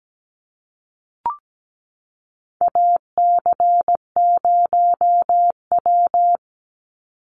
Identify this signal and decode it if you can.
Morse code, heard as a keyed tone, about 17 wpm, 700 Hz. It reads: AC0W
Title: Call Signs Repeated 2x - 17wpm